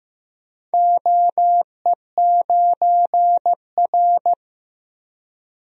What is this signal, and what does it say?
Morse code, heard as a keyed tone, about 15 wpm, 700 Hz. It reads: OE9R
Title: DXCC-100 Call Signs - 4 Characters - Repeated 3x - 15wpm